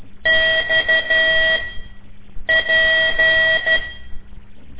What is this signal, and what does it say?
Morse code, heard as a keyed tone, about 12 wpm, spaced as for 7 wpm, 1950 Hz, with noise. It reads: XP
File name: ik1weg2 / CORSI CW / lez5-1